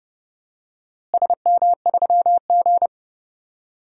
Morse code, heard as a keyed tone, about 30 wpm, 700 Hz. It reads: SM3G
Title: DXCC-100 Call Signs - 4 Characters - Repeated 2x - 30wpm